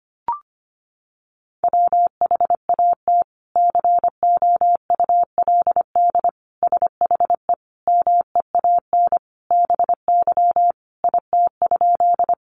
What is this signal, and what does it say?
Morse code, heard as a keyed tone, about 25 wpm, 700 Hz. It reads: WHAT COULD SHE MEAN BY IT?
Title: Sentences from Top 200 Words 25wpm